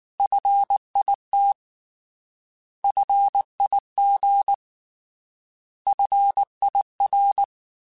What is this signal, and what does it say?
Morse code, heard as a keyed tone, about 19 wpm, 800 Hz. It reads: FIT FIG FIR